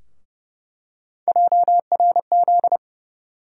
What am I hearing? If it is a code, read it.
Morse code, heard as a keyed tone, about 30 wpm, 700 Hz. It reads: JRZ
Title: USA Call Sign Postfixes - 3 Letters 30wpm